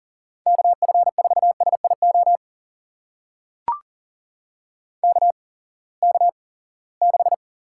Morse code, heard as a keyed tone, about 39 wpm, spaced as for 23 wpm, 700 Hz. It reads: KF4SIO KK6
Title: USA Call Signs - 6 Characters - 8x Character Spacing - 40wpm